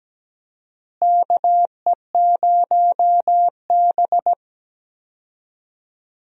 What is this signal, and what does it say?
Morse code, heard as a keyed tone, about 17 wpm, 700 Hz. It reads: KE0B
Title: USA Call Signs - 4 Characters - Repeated 3x - 17wpm